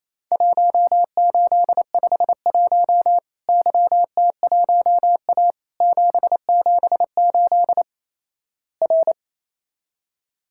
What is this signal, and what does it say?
Morse code, heard as a keyed tone, about 28 wpm, 700 Hz. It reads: 1851 YT1A 778 R
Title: Worked All Europe DX Contest - QTC Exchange - 28wpm